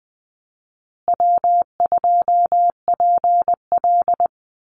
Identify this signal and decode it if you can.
Morse code, heard as a keyed tone, about 20 wpm, 700 Hz. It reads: W2PL
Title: Easy Call Signs Repeated 3x - 20wpm